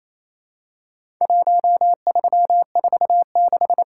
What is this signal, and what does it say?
Morse code, heard as a keyed tone, about 27 wpm, 700 Hz. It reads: 1346
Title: Worked All Europe DX Contest - QTC Exchange - 28wpm